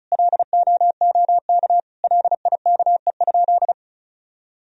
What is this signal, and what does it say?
Morse code, heard as a keyed tone, about 36 wpm, 700 Hz. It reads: LOOK LIKE?
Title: Sentences from Top 100 Words and Abbreviations 35wpm